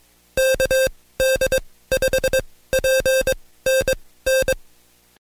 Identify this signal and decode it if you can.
Morse code, heard as a keyed tone, about 21 wpm, spaced as for 16 wpm, 530 Hz. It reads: KD5PNN
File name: Code Quick / AudioDownld2 / Seg134-Stepper7 15WPM